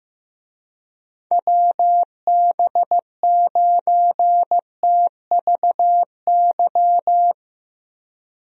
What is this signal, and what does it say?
Morse code, heard as a keyed tone, about 15 wpm, 700 Hz. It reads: WB9TVY
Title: USA Call Signs - 6 Characters - Repeated 2x - 15wpm